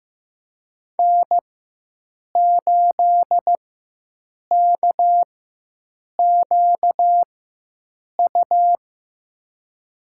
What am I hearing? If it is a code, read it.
Morse code, heard as a keyed tone, about 15 wpm, spaced as for 7 wpm, 700 Hz. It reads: N8KQU
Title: USA Call Signs - 5 Characters - 4x Character Spacing - 15wpm